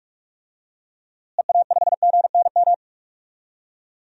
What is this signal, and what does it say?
Morse code, heard as a keyed tone, about 42 wpm, 700 Hz. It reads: EA5GNM
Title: DXCC-100 Call Signs - 6 Characters - Repeated 2x - 45wpm